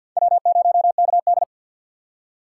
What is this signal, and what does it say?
Morse code, heard as a keyed tone, about 52 wpm, 700 Hz. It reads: W0KD
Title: USA Call Signs - 4 Characters - 2x Character Spacing - 50wpm